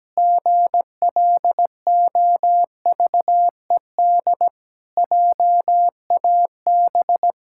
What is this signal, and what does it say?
Morse code, heard as a keyed tone, about 17 wpm, 700 Hz. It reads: GLOVED JAB
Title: Pangram Challenge v2 - 17wpm